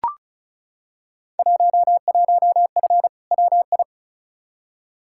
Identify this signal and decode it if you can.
Morse code, heard as a keyed tone, about 35 wpm, 690 Hz. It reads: 11F WI
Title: Field Day Exchanges Encore - 35wpm